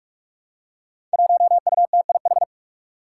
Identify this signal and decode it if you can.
Morse code, heard as a keyed tone, about 44 wpm, 700 Hz. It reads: 1UTIH
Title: Five Challenging Letter-Number Combinations 45wpm